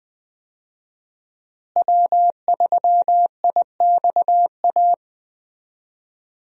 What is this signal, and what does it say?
Morse code, heard as a keyed tone, about 21 wpm, 700 Hz. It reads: W3IXA
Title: USA Call Signs - 5 Characters - Repeated 3x - 20wpm